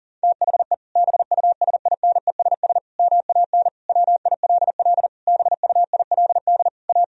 Morse code, heard as a keyed tone, about 40 wpm, 700 Hz. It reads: THE BUSINESS MAN WILL BUILD A
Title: Sentences from Top 1000 Words - Review Part 1 - 40wpm